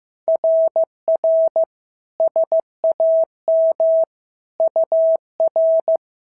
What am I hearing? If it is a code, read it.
Morse code, heard as a keyed tone, about 14 wpm, 650 Hz. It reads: RR SAM UR
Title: SOTA QSO Practice 15wpm